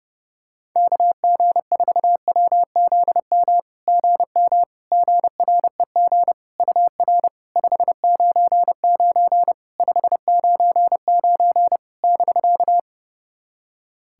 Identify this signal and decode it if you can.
Morse code, heard as a keyed tone, about 30 wpm, 700 Hz. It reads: KG4WZM GM GREG UR 599 599 <BK>
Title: SOTA QSO Practice 30wpm